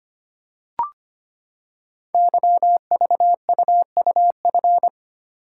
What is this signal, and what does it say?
Morse code, heard as a keyed tone, about 25 wpm, 700 Hz. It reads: YVUUF
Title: Code Groups - 5 Letter:Numbers - 25wpm